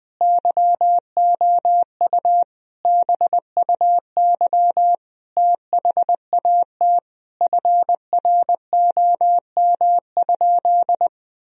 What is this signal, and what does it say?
Morse code, heard as a keyed tone, about 20 wpm, 700 Hz. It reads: YOU BUY THAT FROM?
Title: Sentences from Top 1000 Words - Review Part 4 - 20wpm